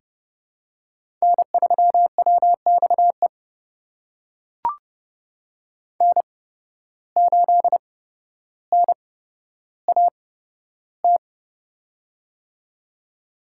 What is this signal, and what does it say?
Morse code, heard as a keyed tone, about 30 wpm, spaced as for 13 wpm, 700 Hz. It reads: N3WXE N8NAT
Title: USA Call Signs - 8x Character Spacing - 30wpm